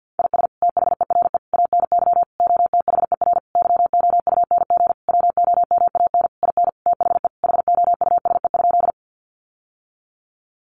Morse code, heard as a keyed tone, about 50 wpm, 700 Hz. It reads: IS THERE ANY OTHER YOUNG WOMAN IN THE HOUSE?
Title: Sentences from Top 1100 Words 50wpm